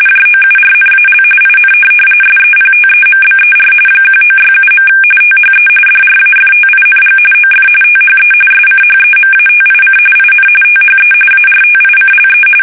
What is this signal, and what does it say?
RTTY, about 45.45 baud. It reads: JOURNAL IN SEPTEMBER 1967. THE DRAFTING OF THE SCHEMATIC IN THE ARTICLE
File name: fsk850 / teletype-RTTY